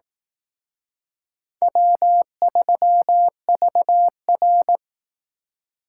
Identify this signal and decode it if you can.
Morse code, heard as a keyed tone, about 18 wpm, 700 Hz. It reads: W3VR
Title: Speed Racing - USA 1x2 Call Signs 15wpm